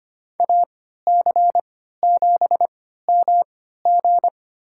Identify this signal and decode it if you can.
Morse code, heard as a keyed tone, about 25 wpm, spaced as for 14 wpm, 700 Hz. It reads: AC7MG
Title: USA Call Signs - 5 Characters - 3x Character Spacing - 25wpm